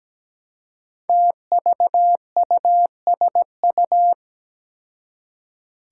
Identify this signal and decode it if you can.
Morse code, heard as a keyed tone, about 17 wpm, 700 Hz. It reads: TVUSU
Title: Five Challenging Letter-Number Combinations 17wpm